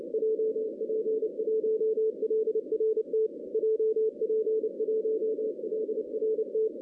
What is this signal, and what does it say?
Morse code, heard as a keyed tone, about 30 wpm, 470 Hz. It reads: JP1LRT JP1LRT
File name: 2016 arrl dx cw / 2016-ARRL-DX-CW-K5TR- - 0-26